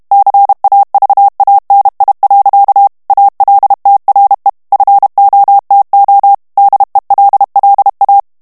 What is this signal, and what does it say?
Morse code, heard as a keyed tone, about 32 wpm, 800 Hz. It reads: CAVANI. ALTRE FOTO DELLA
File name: ITALIEN texte clair 1 vitesse 30 mots minute